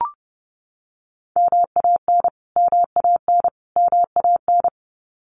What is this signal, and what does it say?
Morse code, heard as a keyed tone, about 30 wpm, 700 Hz. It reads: MAN MAN MAN
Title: Top 200 Words Repeated 30wpm